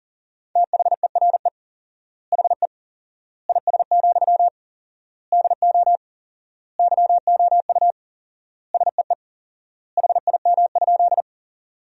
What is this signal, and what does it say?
Morse code, heard as a keyed tone, about 40 wpm, 700 Hz. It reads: THERE HE IS, DO YOU SEE HIM?
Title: 4x Word-Spacing - Sentences from Top 100 Words - 40wpm